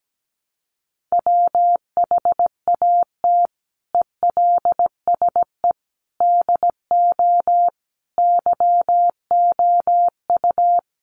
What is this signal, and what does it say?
Morse code, heard as a keyed tone, about 17 wpm, 700 Hz. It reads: WHAT ELSE DO YOU
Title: Sentences from Top 1100 Words 17wpm